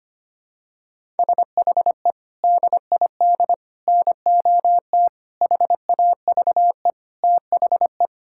Morse code, heard as a keyed tone, about 25 wpm, 700 Hz. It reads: SHE DID NOT HAVE THE
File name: practice / 25wpm / Sentences with 4-Word Phrases - 25wpm